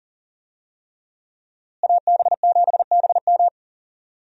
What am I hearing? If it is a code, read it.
Morse code, heard as a keyed tone, about 37 wpm, 700 Hz. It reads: AB7BM
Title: USA Call Signs - 5 Characters - Repeated 3x - 40wpm